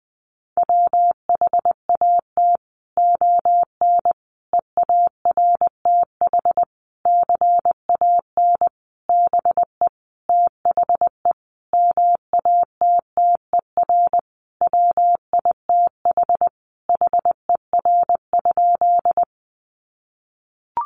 Morse code, heard as a keyed tone, about 20 wpm, 700 Hz. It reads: WHAT ON EARTH CAN BE THE MATTER WITH HER?
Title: Sentences from Top 700 Words 20wpm